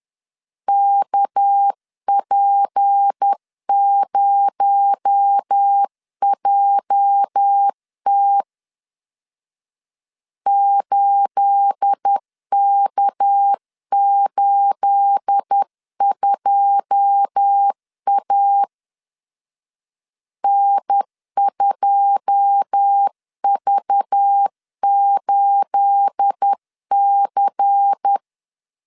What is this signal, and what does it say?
Morse code, heard as a keyed tone, about 10 wpm, 800 Hz. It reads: KP0JT 8K82A N2V8C
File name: LEZIONI / 9 28 10wpm 2842